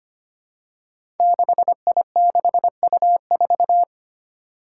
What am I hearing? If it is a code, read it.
Morse code, heard as a keyed tone, about 25 wpm, 700 Hz. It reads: 6I6U4
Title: Five Challenging Letter-Number Combinations 25wpm